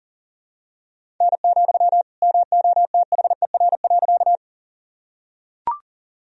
Morse code, heard as a keyed tone, about 40 wpm, 700 Hz. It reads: N, MOTHER.
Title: Sentences from Top 200 Words and Abbreviations 40wpm